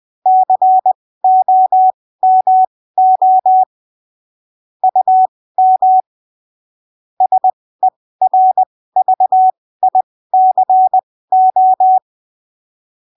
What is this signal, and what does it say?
Morse code, heard as a keyed tone, about 20 wpm, spaced as for 15 wpm, 750 Hz. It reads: COMO UM SERVICO
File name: arquivos / texto / texto 01 20ppm